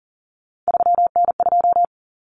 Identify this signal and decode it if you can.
Morse code, heard as a keyed tone, about 40 wpm, 700 Hz. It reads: 3N2
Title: Common Cut Numbers - 3-Digit 40wpm